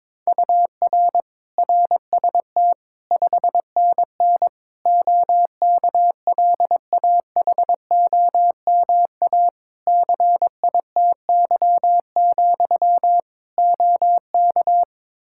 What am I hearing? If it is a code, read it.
Morse code, heard as a keyed tone, about 22 wpm, 700 Hz. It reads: UR RST 5NN OKLAHOMA CITY, OK
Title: Practice QSO - Exchange 1 - Beginning - 22wpm